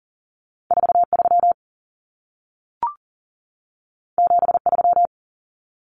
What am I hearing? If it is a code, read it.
Morse code, heard as a keyed tone, about 40 wpm, 700 Hz. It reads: 43 73
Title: Common Cut Numbers - 2-Digit 40wpm